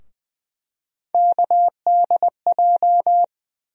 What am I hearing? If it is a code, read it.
Morse code, heard as a keyed tone, about 20 wpm, 700 Hz. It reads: KDJ
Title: USA Call Sign Postfixes - 3 Letters 20wpm